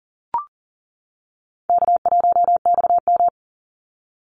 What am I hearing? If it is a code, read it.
Morse code, heard as a keyed tone, about 40 wpm, 700 Hz. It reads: K1XM
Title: Easy Call Signs - 40wpm